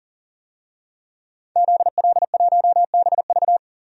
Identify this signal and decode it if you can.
Morse code, heard as a keyed tone, about 40 wpm, 700 Hz. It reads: ZL1BV
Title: DXCC-100 Call Signs - 5 Characters - 40wpm